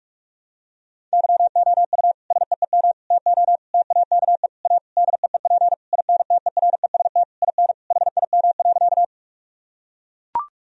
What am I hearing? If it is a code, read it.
Morse code, heard as a keyed tone, about 41 wpm, 700 Hz. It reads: YOU SEEM TO TAKE A DEEP INTEREST IN HIM.
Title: Sentences from Top 1000 Words - Review Part 3 - 45wpm